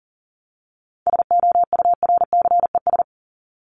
Morse code, heard as a keyed tone, about 40 wpm, 700 Hz. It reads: SOURCES
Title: ES:S Suffix - Top 100 Words with Base Words 40wpm